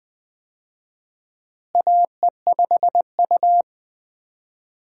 Morse code, heard as a keyed tone, about 20 wpm, 700 Hz. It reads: AE5U
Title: USA Call Signs - 4 Characters - Repeated 2x - 20wpm